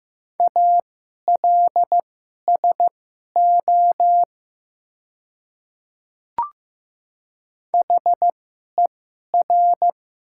Morse code, heard as a keyed tone, about 15 wpm, spaced as for 11 wpm, 700 Hz. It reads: ALSO HER
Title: 2x Character Spacing - Top 200 Words - 15wpm